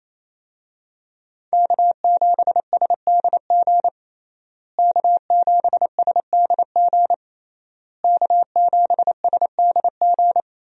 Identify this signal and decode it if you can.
Morse code, heard as a keyed tone, about 28 wpm, 700 Hz. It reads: K7SDG K7SDG K7SDG
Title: USA Call Signs - 5 Characters - Repeated 3x - 28wpm